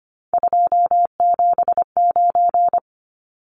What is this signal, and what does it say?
Morse code, heard as a keyed tone, about 25 wpm, 700 Hz. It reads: 279
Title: Numbers - 3-Digit 25wpm